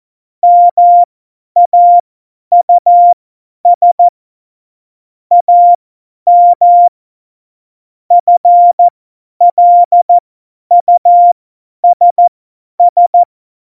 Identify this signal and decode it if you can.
Morse code, heard as a keyed tone, about 14 wpm, spaced as for 10 wpm, 700 Hz. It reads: MAUS AM FLUSS
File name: Morsekurs / Witze / Witz 11. Dezember, 50 F70